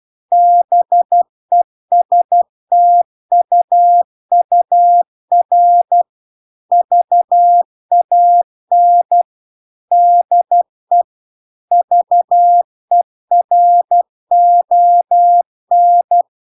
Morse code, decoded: BESTUUR VAN DE VERON